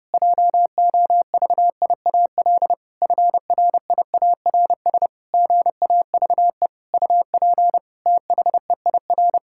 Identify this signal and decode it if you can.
Morse code, heard as a keyed tone, about 30 wpm, 700 Hz. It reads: JOVIAL FRIARS GAVE UP THEIR
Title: Pangram Challenge v2 - 30wpm